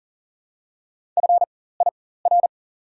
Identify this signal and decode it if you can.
Morse code, heard as a keyed tone, about 37 wpm, spaced as for 18 wpm, 700 Hz. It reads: FIR
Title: 4x Character Spacing - Top 500 Words - 40wpm